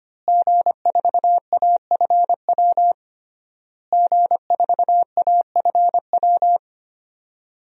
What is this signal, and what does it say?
Morse code, heard as a keyed tone, about 25 wpm, 700 Hz. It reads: G4AFW G4AFW
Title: DXCC-100 Call Signs - 5 Characters - Repeated 3x - 25wpm